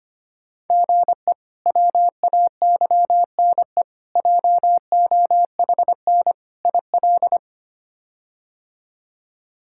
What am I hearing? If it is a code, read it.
Morse code, heard as a keyed tone, about 25 wpm, 700 Hz. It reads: GE WAYNE JOHN IL
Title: K1USN SST QSO Practice - 25wpm